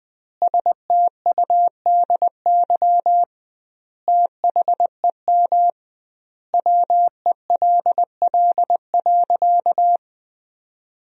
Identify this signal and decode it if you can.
Morse code, heard as a keyed tone, about 20 wpm, 700 Hz. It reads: STUDY THEM WELL.
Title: 2x Word-Spacing - Sentences from Top 300 Words - 20wpm